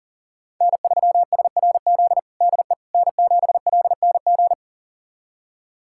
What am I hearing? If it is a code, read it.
Morse code, heard as a keyed tone, about 40 wpm, 700 Hz. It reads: N3SRZ DE N7LNG
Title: USA Callsign DE Callsign - 5 Characters - 40wpm